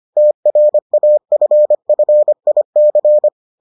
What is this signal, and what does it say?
Morse code, heard as a keyed tone, about 25 wpm, 600 Hz. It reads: TRAFFIC